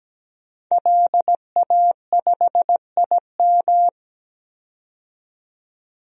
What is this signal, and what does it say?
Morse code, heard as a keyed tone, about 17 wpm, 700 Hz. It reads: LA5IM
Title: DXCC-100 Call Signs - 5 Characters - 17wpm